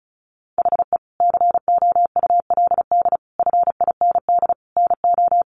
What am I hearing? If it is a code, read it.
Morse code, heard as a keyed tone, about 34 wpm, 700 Hz. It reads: HE COULD FIND NO
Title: Sentences from Top 1100 Words 35wpm